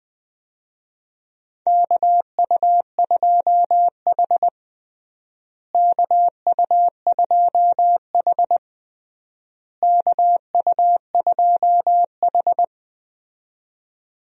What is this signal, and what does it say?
Morse code, heard as a keyed tone, about 21 wpm, 700 Hz. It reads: KU2H KU2H KU2H
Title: USA Call Signs - 4 Characters - Repeated 3x - 20wpm